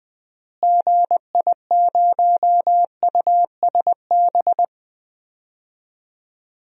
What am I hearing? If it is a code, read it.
Morse code, heard as a keyed tone, about 20 wpm, 700 Hz. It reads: GI0USB
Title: DXCC-100 Call Signs - 6 Characters - Repeated 3x - 20wpm